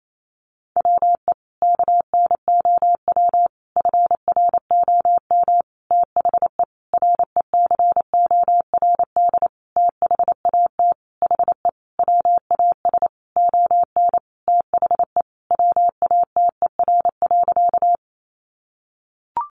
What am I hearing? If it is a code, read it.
Morse code, heard as a keyed tone, about 28 wpm, 700 Hz. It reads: WE KNOW FROM THE RECORD THAT HE WAS ON THE WATER.
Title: Sentences from Top 500 Words 28wpm